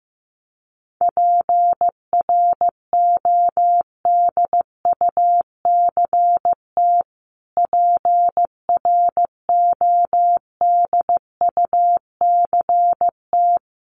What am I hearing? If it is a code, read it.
Morse code, heard as a keyed tone, about 15 wpm, 700 Hz. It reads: PRODUCT PRODUCT
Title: Top 500 Words Repeated 15wpm